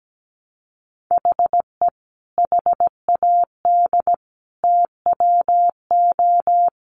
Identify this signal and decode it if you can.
Morse code, heard as a keyed tone, about 17 wpm, 700 Hz. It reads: HE HAD TWO
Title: Sentences from Top 300 Words 17wpm